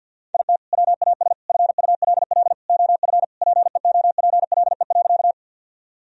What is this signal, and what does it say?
Morse code, heard as a keyed tone, about 52 wpm, 700 Hz. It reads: IT WAS FULL OF PEOPLE.
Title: Sentences from Top 1000 Words - Review Part 3 - 50wpm